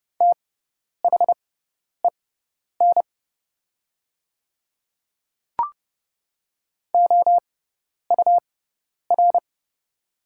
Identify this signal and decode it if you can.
Morse code, heard as a keyed tone, about 30 wpm, spaced as for 10 wpm, 700 Hz. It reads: THEN OUR
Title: 6x Character Spacing - Top 300 Words - 30wpm